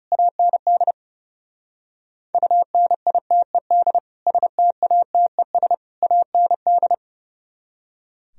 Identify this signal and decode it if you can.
Morse code, heard as a keyed tone, about 39 wpm, 700 Hz. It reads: AND UNITED STATES AND